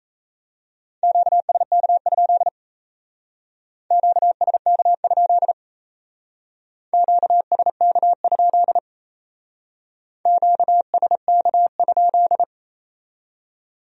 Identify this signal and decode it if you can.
Morse code, heard as a keyed tone, about 35 wpm, 700 Hz. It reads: QSK? QSK? QSK? QSK?
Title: Speed-Racing Q-Codes - 28wpm